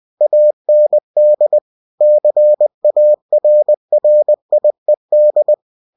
Morse code, heard as a keyed tone, about 20 wpm, 600 Hz. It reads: AND CARRIED
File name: cw-books / winnie-the-pooh / winnie-20wpm-page-0032